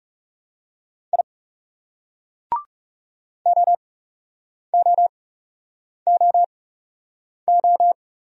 Morse code, heard as a keyed tone, about 40 wpm, spaced as for 8 wpm, 700 Hz. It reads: I OOOO